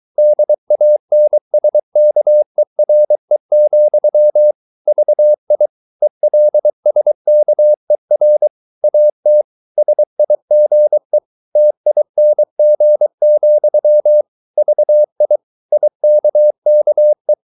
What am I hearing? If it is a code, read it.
Morse code, heard as a keyed tone, about 23 wpm, 600 Hz. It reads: DANSKERE, VI ELSKER AT SIGE TING, VI IKKE
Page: Ferie 23 wpm